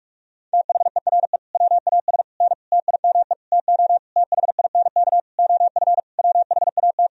Morse code, heard as a keyed tone, about 41 wpm, 700 Hz. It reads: THERE WAS N TIME TO THINK OF WHAT
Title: Sentences from Top 400 Words and Abbreviations 45wpm